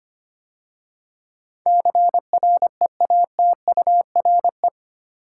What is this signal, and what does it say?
Morse code, heard as a keyed tone, about 25 wpm, 700 Hz. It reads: CREATURE
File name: 8 Letter Words - First 3 Letters Given - 25wpm